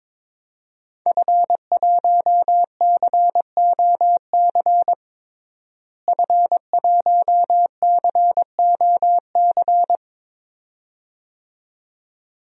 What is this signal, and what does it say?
Morse code, heard as a keyed tone, about 22 wpm, 700 Hz. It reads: F1COC F1COC
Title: DXCC-100 Call Signs - 5 Characters - Repeated 2x - 22wpm